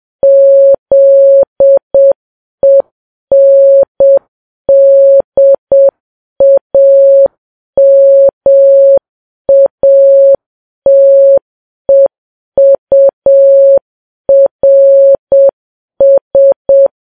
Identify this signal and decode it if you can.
Morse code, heard as a keyed tone, about 7 wpm, 550 Hz. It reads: ZENDAMATEURS
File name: Files Web tls / Morse on9fv / morse7kps